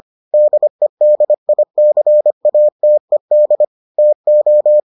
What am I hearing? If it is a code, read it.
Morse code, heard as a keyed tone, about 25 wpm, 600 Hz. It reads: DEDICATED TO